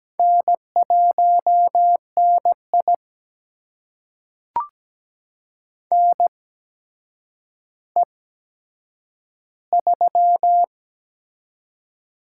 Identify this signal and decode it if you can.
Morse code, heard as a keyed tone, about 17 wpm, spaced as for 8 wpm, 700 Hz. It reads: N1NI NE3
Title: USA Call Signs - 4 Characters - 8x Character Spacing - 17wpm